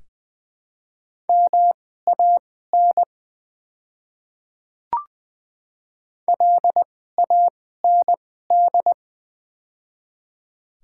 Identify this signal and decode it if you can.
Morse code, heard as a keyed tone, about 20 wpm, spaced as for 14 wpm, 700 Hz. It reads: MAN LAND
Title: 2x Character Spacing - Top 200 Words - 20wpm